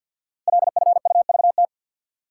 Morse code, heard as a keyed tone, about 52 wpm, 700 Hz. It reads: LFUVT
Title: Code Groups - 5 Letters - 50wpm